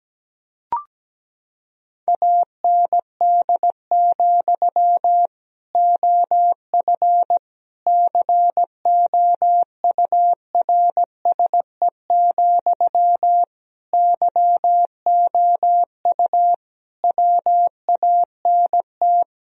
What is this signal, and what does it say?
Morse code, heard as a keyed tone, about 17 wpm, 700 Hz. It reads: AND, OF COURSE, YOU WANT
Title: Sentences from Top 1000 Words - Review Part 1 - 17wpm